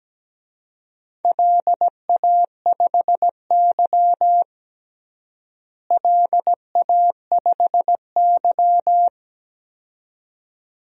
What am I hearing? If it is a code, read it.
Morse code, heard as a keyed tone, about 17 wpm, 700 Hz. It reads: LA5Y LA5Y
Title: DXCC-100 Call Signs - 4 Characters - Repeated 2x - 17wpm